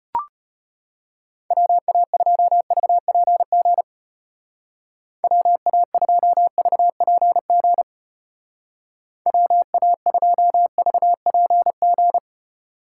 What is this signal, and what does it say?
Morse code, heard as a keyed tone, about 32 wpm, 700 Hz. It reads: WA2VPG WA2VPG WA2VPG